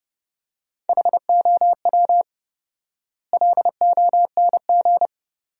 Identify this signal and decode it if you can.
Morse code, heard as a keyed tone, about 30 wpm, 700 Hz. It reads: HOW LONG
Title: 4x Word-Spacing - Sentences from Top 500 Words - 30wpm